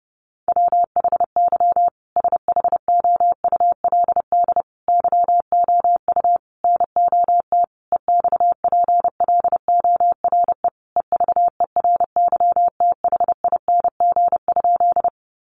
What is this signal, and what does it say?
Morse code, heard as a keyed tone, about 30 wpm, 700 Hz. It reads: WHY SHOULD YOU NOT EXPLORE EVERYTHING?
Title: Sentences from Top 1100 Words 30wpm